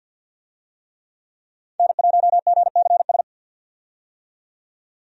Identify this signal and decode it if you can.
Morse code, heard as a keyed tone, about 49 wpm, 700 Hz. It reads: N1GCS
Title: USA Call Signs - 5 Characters - 50wpm